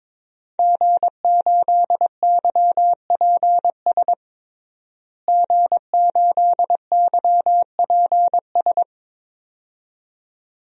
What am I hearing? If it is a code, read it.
Morse code, heard as a keyed tone, about 22 wpm, 700 Hz. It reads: G8YPS G8YPS
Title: DXCC-100 Call Signs - 5 Characters - Repeated 2x - 22wpm